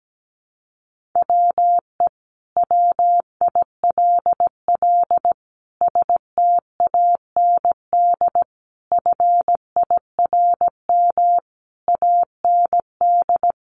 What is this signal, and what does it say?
Morse code, heard as a keyed tone, about 17 wpm, 700 Hz. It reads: WE WILL STAND FIRM AND
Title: Sentences from Top 1400 Words 17wpm